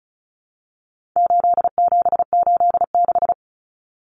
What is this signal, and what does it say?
Morse code, heard as a keyed tone, about 34 wpm, 700 Hz. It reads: 8786
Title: Numbers - 4-Digit 35wpm